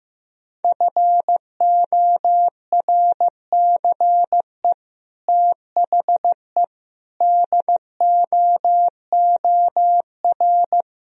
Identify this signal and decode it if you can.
Morse code, heard as a keyed tone, about 16 wpm, 700 Hz. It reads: FORCE THE DOOR
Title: Sentences from Top 1000 Words - Review Part 3 - 15wpm